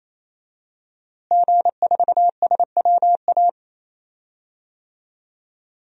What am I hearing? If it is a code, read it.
Morse code, heard as a keyed tone, about 28 wpm, 700 Hz. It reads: G4SWA
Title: DXCC-100 Call Signs - 5 Characters - Repeated 3x - 28wpm